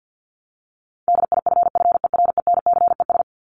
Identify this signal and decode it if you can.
Morse code, heard as a keyed tone, about 50 wpm, 700 Hz. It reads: DIFFERENCES
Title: ES:S Suffix - Top 100 Words with Base Words 50wpm